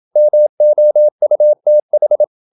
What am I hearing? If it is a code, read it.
Morse code, heard as a keyed tone, about 27 wpm, 600 Hz. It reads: MOUTH
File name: Never 0027wpm